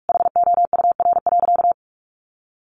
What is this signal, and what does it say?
Morse code, heard as a keyed tone, about 44 wpm, 700 Hz. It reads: HOUR.
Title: Sentences from Top 500 Words 45wpm